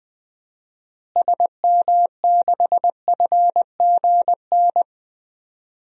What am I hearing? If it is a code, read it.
Morse code, heard as a keyed tone, about 20 wpm, 700 Hz. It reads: SM6FGN
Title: DXCC-100 Call Signs - 6 Characters - Repeated 3x - 20wpm